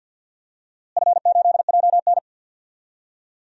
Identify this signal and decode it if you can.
Morse code, heard as a keyed tone, about 47 wpm, 700 Hz. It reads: F8JN
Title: DXCC-100 Call Signs - 4 Characters - Repeated 3x - 50wpm